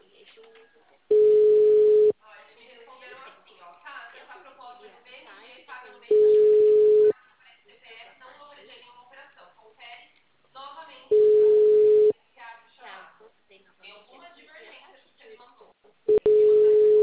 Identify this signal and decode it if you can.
Ringback tone (Europe, 425 Hz), with speech.
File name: out- -2007- - - .331